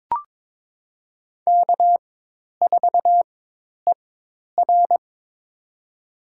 Morse code, heard as a keyed tone, about 22 wpm, spaced as for 10 wpm, 700 Hz. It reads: K4ER
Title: USA Call Signs - 4 Characters - 4x Character Spacing - 22wpm